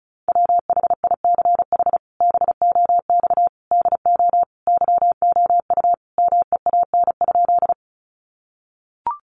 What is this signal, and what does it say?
Morse code, heard as a keyed tone, about 34 wpm, 700 Hz. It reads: WHICH BOX DO YOU MEAN?
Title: Sentences from Top 500 Words 35wpm